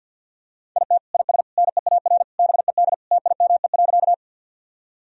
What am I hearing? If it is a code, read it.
Morse code, heard as a keyed tone, about 51 wpm, 700 Hz. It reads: IT IS NEAR BED TIME.
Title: Sentences from Top 1000 Words - Review Part 1 - 50wpm